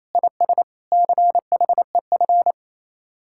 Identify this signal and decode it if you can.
Morse code, heard as a keyed tone, about 38 wpm, spaced as for 33 wpm, 700 Hz. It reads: IS CHEF
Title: Speed-Racing Common QSO Phrases - 28wpm